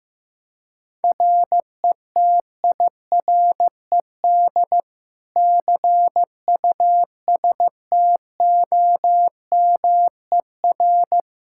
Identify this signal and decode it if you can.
Morse code, heard as a keyed tone, about 15 wpm, 700 Hz. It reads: RETIRED CUSTOMER
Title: Practice QSO - Exchange 3 - More Info - 15wpm